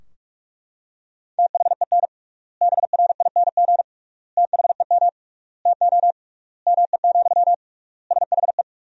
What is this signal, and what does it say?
Morse code, heard as a keyed tone, about 41 wpm, 700 Hz. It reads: THEN BRING THEM TO ME, SHE
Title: 3x Word-Spacing - Sentences from Top 500 Words - 45wpm